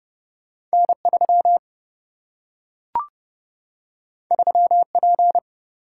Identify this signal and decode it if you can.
Morse code, heard as a keyed tone, about 30 wpm, 700 Hz. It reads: N 3 3 P
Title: Single-Letter Single-Number 30wpm